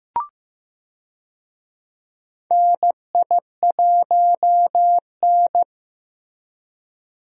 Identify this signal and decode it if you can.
Morse code, heard as a keyed tone, about 15 wpm, 700 Hz. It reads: NI1N
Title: USA Call Signs - 4 Characters - Repeated 3x - 15wpm